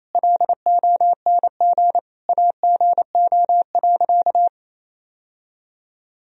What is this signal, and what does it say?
Morse code, heard as a keyed tone, about 28 wpm, 700 Hz. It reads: LONG AGO.
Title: Sentences from Top 1000 Words - Review Part 3 - 28wpm